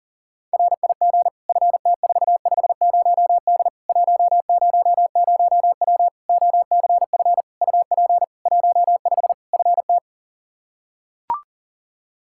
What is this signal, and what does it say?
Morse code, heard as a keyed tone, about 40 wpm, 700 Hz. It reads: RIG FT450D 100W OCF UP 15 FT
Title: Practice QSO - Exchange 2 - Info - 40wpm